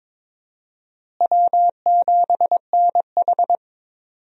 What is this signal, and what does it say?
Morse code, heard as a keyed tone, about 21 wpm, 700 Hz. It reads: W7NH
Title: USA Call Signs - 4 Characters - Repeated 3x - 22wpm